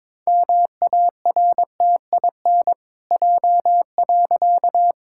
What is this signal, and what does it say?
Morse code, heard as a keyed tone, about 22 wpm, 700 Hz. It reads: MARTIN J.
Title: Pangram Challenge v2 - 22wpm